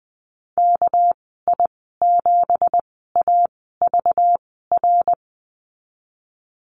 Morse code, heard as a keyed tone, about 19 wpm, spaced as for 14 wpm, 700 Hz. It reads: KI7AVR
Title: 200 Lesson 32g - Letter k - Call Signs 20wpm